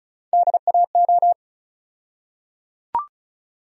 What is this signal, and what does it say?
Morse code, heard as a keyed tone, about 36 wpm, 700 Hz. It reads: DAO
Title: USA Call Sign Postfixes - 3 Letters 35wpm